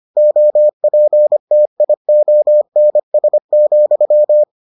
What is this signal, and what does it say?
Morse code, heard as a keyed tone, about 25 wpm, 600 Hz. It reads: OPTIONS,